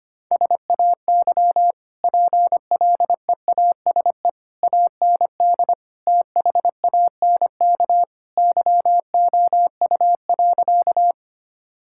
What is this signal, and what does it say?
Morse code, heard as a keyed tone, about 25 wpm, 700 Hz. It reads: SAY PLEASE AND THANK YOU.
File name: Easy Sentences with Hints - 25wpm